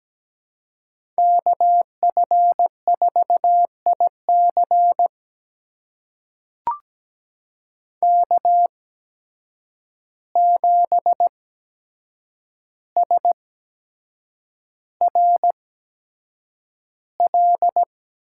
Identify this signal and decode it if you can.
Morse code, heard as a keyed tone, about 17 wpm, spaced as for 7 wpm, 700 Hz. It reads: KF4IC K7SRL
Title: USA Call Signs - 5 Characters - 8x Character Spacing - 17wpm